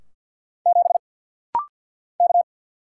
Morse code, heard as a keyed tone, about 50 wpm, 690 Hz. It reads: 7 K
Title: Single Letter-Number Alt - Rapid-Fire 50wpm